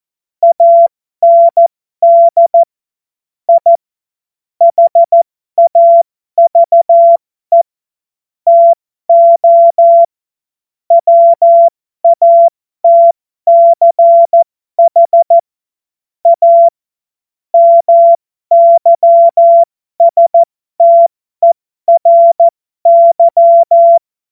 Morse code, heard as a keyed tone, about 14 wpm, spaced as for 12 wpm, 680 Hz. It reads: AND I HAVE TO WATCH A MYSTERY
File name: Morsekurs / Witze